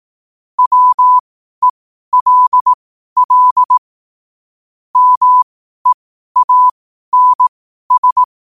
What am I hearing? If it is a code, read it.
Morse code, heard as a keyed tone, about 18 wpm, spaced as for 12 wpm, 1000 Hz. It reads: WELL MEANS
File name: qotdfiles / 12wpm